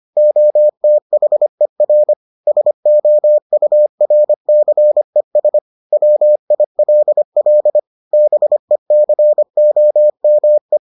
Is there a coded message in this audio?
Yes: Morse code from 0.2 to 10.8 s, heard as a keyed tone, about 25 wpm, 600 Hz. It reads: OTHER SOURCES WILL BECOME